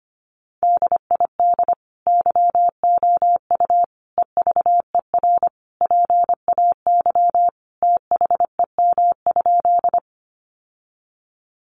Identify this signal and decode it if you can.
Morse code, heard as a keyed tone, about 25 wpm, 700 Hz. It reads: DID YOU EVER PAY THEM?
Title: Sentences from Top 700 Words 25wpm